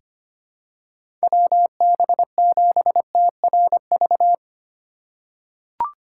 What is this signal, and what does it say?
Morse code, heard as a keyed tone, about 25 wpm, 700 Hz. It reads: WB7TRV
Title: USA Call Signs - 6x Character Spacing - 25wpm